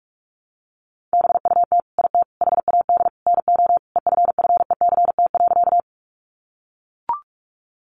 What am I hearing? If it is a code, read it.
Morse code, heard as a keyed tone, about 44 wpm, 700 Hz. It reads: BUT IT HAD NO EFFECT.
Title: Sentences from Top 1000 Words 45wpm